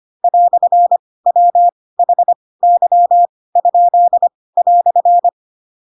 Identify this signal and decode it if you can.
Morse code, heard as a keyed tone, about 25 wpm, spaced as for 18 wpm, 700 Hz. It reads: "WHY?"
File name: Pinocchio / Pinocchio ch4 12 25 ik1hge